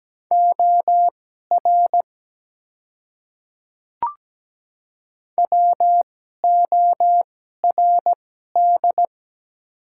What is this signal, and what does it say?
Morse code, heard as a keyed tone, about 17 wpm, spaced as for 12 wpm, 700 Hz. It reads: OR WORD
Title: 2x Character Spacing - Top 200 Words - 17wpm